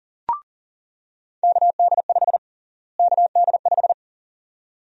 Morse code, heard as a keyed tone, about 37 wpm, 700 Hz. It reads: KD5 KD5
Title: USA Call Sign Prefixes - Repeated 2x - 40wpm